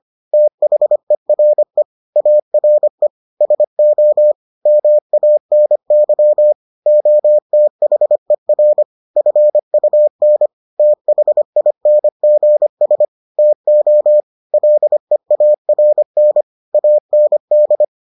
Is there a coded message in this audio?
Yes: Morse code from 0.3 to 17.8 s, heard as a keyed tone, about 25 wpm, 600 Hz. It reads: THERE ARE SO MANY OTHER FUN THINGS TO LEARN AND